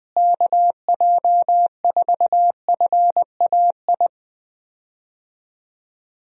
Morse code, heard as a keyed tone, about 20 wpm, 700 Hz. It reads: KJ4FAI
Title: USA Call Signs - 6 Characters - Repeated 3x - 20wpm